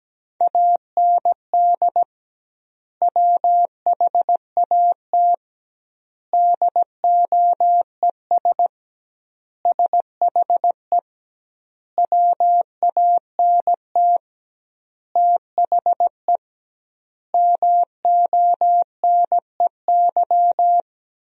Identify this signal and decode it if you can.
Morse code, heard as a keyed tone, about 17 wpm, 700 Hz. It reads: AND WHAT DOES SHE WANT THE MONEY
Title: 2x Word-Spacing - Sentences from Top 400 Words - 17wpm